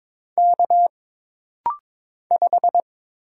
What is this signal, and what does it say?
Morse code, heard as a keyed tone, about 22 wpm, 700 Hz. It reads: K 5
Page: This rapid-fire timing is geared to push your ICR (Instant Character Recognition) skills.